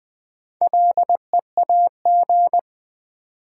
Morse code, heard as a keyed tone, about 20 wpm, 690 Hz. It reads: LEAG
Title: Build A Word - 2 Letters At A Time - 6-Letter Words - 20wpm